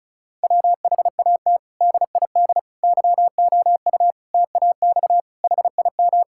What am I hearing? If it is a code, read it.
Morse code, heard as a keyed tone, about 36 wpm, 700 Hz. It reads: WHAT DID YOU TAX HIM
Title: Sentences from Top 1000 Words - Review Part 5 - 35wpm